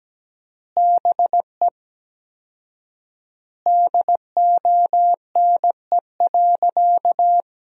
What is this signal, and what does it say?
Morse code, heard as a keyed tone, about 17 wpm, 700 Hz. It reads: BE DONE.
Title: 4x Word-Spacing - Sentences from Top 500 Words - 17wpm